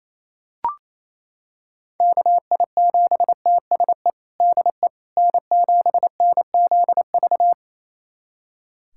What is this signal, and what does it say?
Morse code, heard as a keyed tone, about 27 wpm, 700 Hz. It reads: KI7TSE DE N7NZV
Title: USA Callsign DE Callsign 28wpm